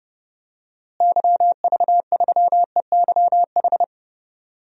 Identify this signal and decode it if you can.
Morse code, heard as a keyed tone, about 30 wpm, 700 Hz. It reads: YV3EYH
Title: DXCC-100 Call Signs - 6 Characters - Repeated 2x - 30wpm